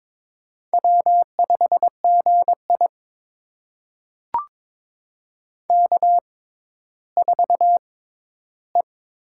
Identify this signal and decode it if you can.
Morse code, heard as a keyed tone, about 22 wpm, spaced as for 13 wpm, 700 Hz. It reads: W5GI K4E
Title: USA Call Signs - 4 Characters - 6x Character Spacing - 22wpm